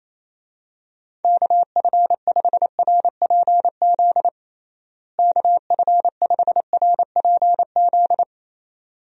Morse code, heard as a keyed tone, about 28 wpm, 700 Hz. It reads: KF5RPZ KF5RPZ
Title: USA Call Signs - 6 Characters - Repeated 2x - 28wpm